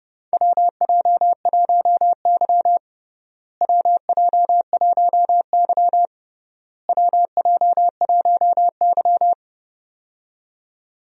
Morse code, heard as a keyed tone, about 32 wpm, 700 Hz. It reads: WJ1Y WJ1Y WJ1Y
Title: USA Call Signs - 4 Characters - Repeated 3x - 30wpm